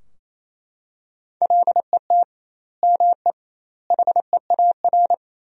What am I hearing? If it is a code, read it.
Morse code, heard as a keyed tone, about 28 wpm, 700 Hz. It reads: LET ME HEAR
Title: 2x Word-Spacing - Sentences from Top 300 Words - 28wpm